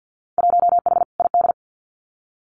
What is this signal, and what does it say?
Morse code, heard as a keyed tone, about 50 wpm, 700 Hz. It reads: 1H ID
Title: Winter Field Day Exchanges 50wpm